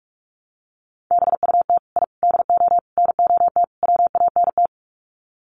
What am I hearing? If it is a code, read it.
Morse code, heard as a keyed tone, about 44 wpm, 700 Hz. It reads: BUT I DO NOT WANT
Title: Sets of 5 Words 45wpm